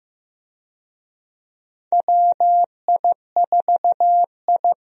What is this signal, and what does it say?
Morse code, heard as a keyed tone, about 15 wpm, 700 Hz. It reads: WI4I
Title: USA Call Signs - 4 Characters - Repeated 3x - 15wpm